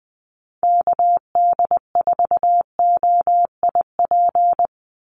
Morse code, heard as a keyed tone, about 20 wpm, 700 Hz. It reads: KD4OIP
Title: USA Call Signs 20wpm